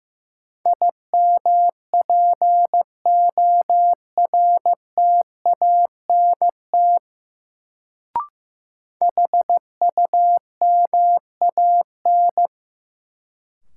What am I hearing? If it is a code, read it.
Morse code, heard as a keyed tone, about 15 wpm, 700 Hz. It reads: IMPORTANT HUMAN
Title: Sentences from Top 600 Words - New Word Drill - 15wpm